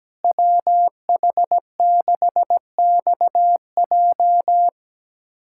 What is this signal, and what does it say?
Morse code, heard as a keyed tone, about 17 wpm, 700 Hz. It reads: WH6XJ
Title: USA Callsign DE Callsign - 5 Characters - 17wpm